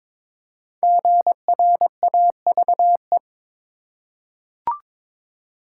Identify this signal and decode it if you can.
Morse code, heard as a keyed tone, about 23 wpm, 700 Hz. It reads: GRAVE
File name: 5 Letter Words - First 3 Letters Given - 22wpm